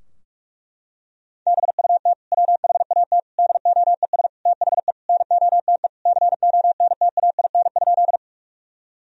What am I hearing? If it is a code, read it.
Morse code, heard as a keyed tone, about 46 wpm, 700 Hz. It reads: BUT WHAT DOES THE NOTE CONTAIN?
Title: Sentences from Top 1000 Words - Review Part 2 - 45wpm